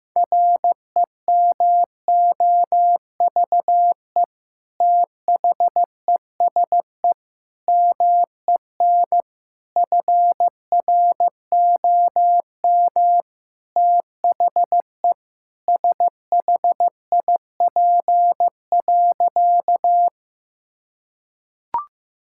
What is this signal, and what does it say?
Morse code, heard as a keyed tone, about 16 wpm, 700 Hz. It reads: REMOVE THESE MEN FROM THE SHIP.
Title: Sentences from Top 1000 Words - Review Part 5 - 15wpm